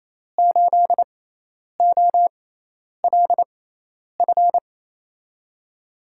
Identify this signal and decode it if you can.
Morse code, heard as a keyed tone, about 28 wpm, spaced as for 10 wpm, 700 Hz. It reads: 8OLF
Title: USA Call Signs - 5 Characters - 6x Character Spacing - 28wpm